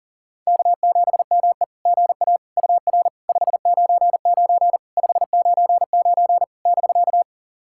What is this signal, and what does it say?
Morse code, heard as a keyed tone, about 40 wpm, 690 Hz. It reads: K7ME GA UR 599 599 <BK>
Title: SOTA QSO Practice 40wpm